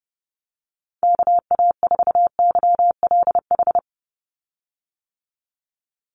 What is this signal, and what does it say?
Morse code, heard as a keyed tone, about 30 wpm, 700 Hz. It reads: KA4YLH
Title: USA Call Signs 30wpm